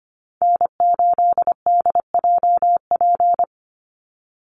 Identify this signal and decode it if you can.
Morse code, heard as a keyed tone, about 25 wpm, 700 Hz. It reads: N8DJP
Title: USA Call Signs 25wpm